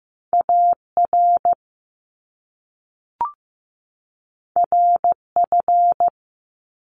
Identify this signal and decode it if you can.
Morse code, heard as a keyed tone, about 15 wpm, 700 Hz. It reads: AR RF
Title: Top Two Letter Combinations in Words 15wpm